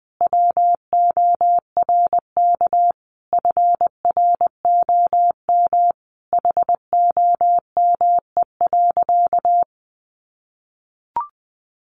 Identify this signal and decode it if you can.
Morse code, heard as a keyed tone, about 20 wpm, 700 Hz. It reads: WORK FROM HOME.
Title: Sentences from Top 1500 Words 20wpm